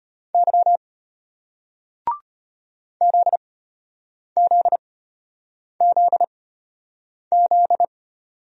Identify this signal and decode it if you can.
Morse code, heard as a keyed tone, about 33 wpm, spaced as for 8 wpm, 700 Hz. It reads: Y ZZZZ